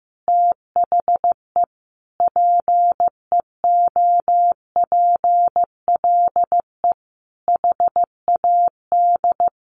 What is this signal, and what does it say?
Morse code, heard as a keyed tone, about 15 wpm, 700 Hz. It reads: THE PEOPLE HAD
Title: Sentences from Top 1400 Words 15wpm